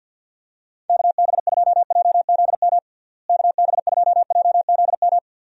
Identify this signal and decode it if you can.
Morse code, heard as a keyed tone, about 47 wpm, 700 Hz. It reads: KB2JZM KB2JZM
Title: USA Call Signs - 6 Characters - Repeated 3x - 50wpm